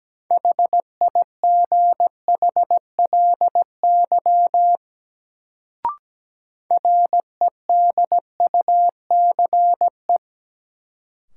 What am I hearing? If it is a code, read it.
Morse code, heard as a keyed tone, about 17 wpm, 700 Hz. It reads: HIGHLY REDUCE
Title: Sentences from Top 1100 Words - New Word Drill - 17wpm